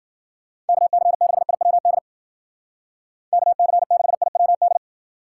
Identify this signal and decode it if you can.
Morse code, heard as a keyed tone, about 49 wpm, 700 Hz. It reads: KC6IWD KC6IWD
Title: Speed Racing - USA 2x3 Call Signs 40wpm